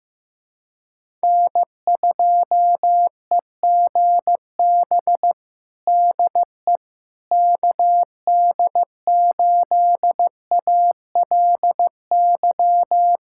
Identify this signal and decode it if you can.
Morse code, heard as a keyed tone, about 15 wpm, 700 Hz. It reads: N2EGB DE KD8ALY
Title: USA Callsign DE Callsign 15wpm